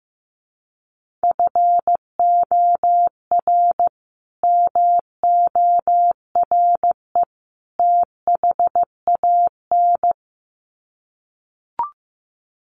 Morse code, heard as a keyed tone, about 15 wpm, 700 Hz. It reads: FOR MORE THAN
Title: Sets of 3 Words 15wpm